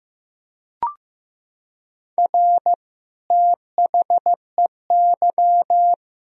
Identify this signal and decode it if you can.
Morse code, heard as a keyed tone, about 15 wpm, 700 Hz. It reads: R THEY
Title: Sentences from Top 200 Words and Abbreviations 15wpm